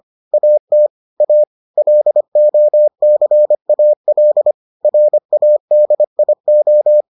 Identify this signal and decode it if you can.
Morse code, heard as a keyed tone, about 25 wpm, 600 Hz. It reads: AT A LOCAL RADIO